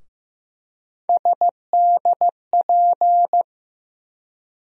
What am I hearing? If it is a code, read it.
Morse code, heard as a keyed tone, about 15 wpm, 700 Hz. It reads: SDP
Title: USA Call Sign Postfixes - 3 Letters 15wpm